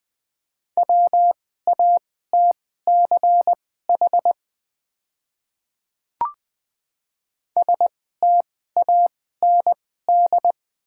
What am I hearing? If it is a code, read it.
Morse code, heard as a keyed tone, about 20 wpm, spaced as for 14 wpm, 700 Hz. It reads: WATCH STAND
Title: 2x Character Spacing - Top 300 Words - 20wpm